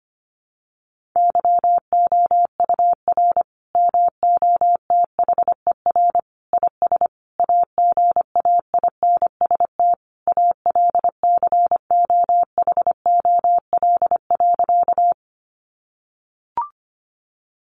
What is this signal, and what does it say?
Morse code, heard as a keyed tone, about 25 wpm, 700 Hz. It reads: YOUR MOTHER IS AGAINST ALCOHOL.
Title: Sentences from Top 1400 Words 25wpm